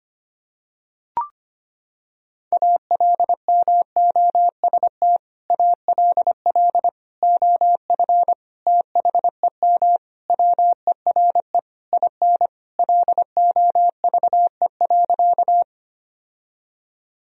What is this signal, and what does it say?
Morse code, heard as a keyed tone, about 25 wpm, 700 Hz. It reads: ALMOST ALL OF THEM WERE IN LOVE.
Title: Sentences from Top 1000 Words - Review Part 4 - 25wpm